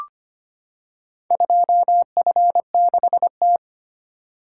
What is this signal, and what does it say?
Morse code, heard as a keyed tone, about 25 wpm, 700 Hz. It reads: 2F6T
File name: practice / 25wpm / Four Letter-Number Combinations - 25wpm